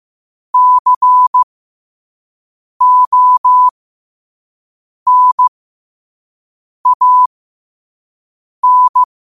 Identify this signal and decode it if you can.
Morse code, heard as a keyed tone, about 15 wpm, spaced as for 5 wpm, 1000 Hz. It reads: CONAN
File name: qotdfiles / 5wpm